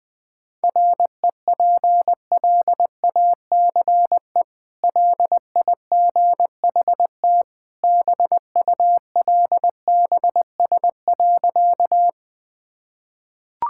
Morse code, heard as a keyed tone, about 20 wpm, 700 Hz. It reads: REPLACE LIGHT BULBS.
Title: Easy Sentences with Hints - 20wpm